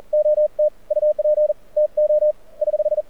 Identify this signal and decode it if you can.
Morse code, heard as a keyed tone, about 46 wpm, 590 Hz. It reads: OT UP TO <HH>
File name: archived / CRSnet